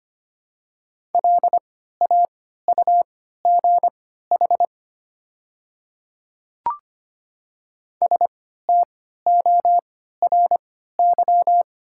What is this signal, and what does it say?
Morse code, heard as a keyed tone, about 25 wpm, spaced as for 14 wpm, 690 Hz. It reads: LAUGH STORY
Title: 3x Character Spacing - Top 500 Words - 25wpm